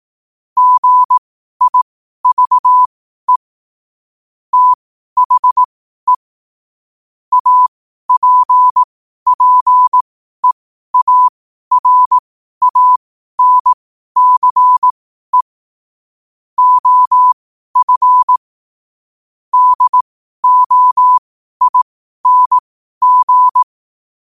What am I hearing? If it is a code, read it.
Morse code, heard as a keyed tone, about 18 wpm, spaced as for 12 wpm, 1000 Hz. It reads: GIVE THE APPEARANCE OF DOING
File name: qotdfiles / 12wpm